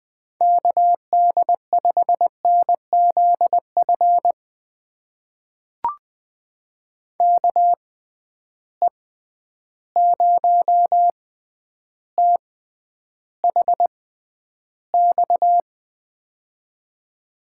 Morse code, heard as a keyed tone, about 20 wpm, spaced as for 10 wpm, 700 Hz. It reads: KD5NZF KE0THX
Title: USA Call Signs - 6 Characters - 6x Character Spacing - 20wpm